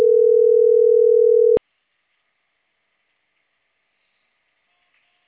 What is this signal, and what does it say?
Ringback tone (North America), 440 + 480 Hz.